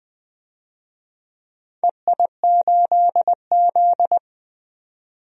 Morse code, heard as a keyed tone, about 20 wpm, 700 Hz. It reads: EI8Z
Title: DXCC-100 Call Signs - 4 Characters - 20wpm